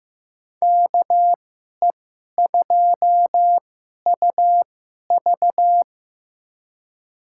Morse code, heard as a keyed tone, about 15 wpm, spaced as for 11 wpm, 700 Hz. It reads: KE2UV
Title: USA Call Signs - 5 Characters - 2x Character Spacing - 15wpm